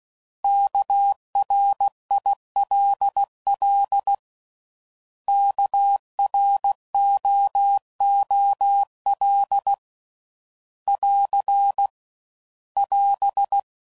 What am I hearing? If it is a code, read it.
Morse code, heard as a keyed tone, about 16 wpm, 800 Hz. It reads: KRILL KROOL <AR> <AS>
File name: Prefixes / By First Letter / Ks